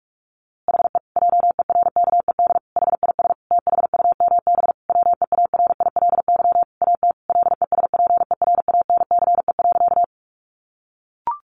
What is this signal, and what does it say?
Morse code, heard as a keyed tone, about 44 wpm, 700 Hz. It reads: HE JERKED HIS THUMB WEARILY AT LESPERANCE.
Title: A Sound Of Thunder 45wpm